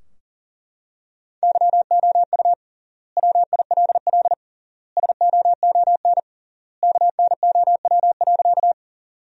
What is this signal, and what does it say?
Morse code, heard as a keyed tone, about 40 wpm, 700 Hz. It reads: YOU WILL SOON KNOW.
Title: 3x Word-Spacing - Sentences from Top 400 Words - 40wpm